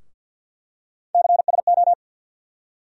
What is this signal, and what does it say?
Morse code, heard as a keyed tone, about 52 wpm, 700 Hz. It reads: CSO
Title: USA Call Sign Postfixes - 3 Letters 50wpm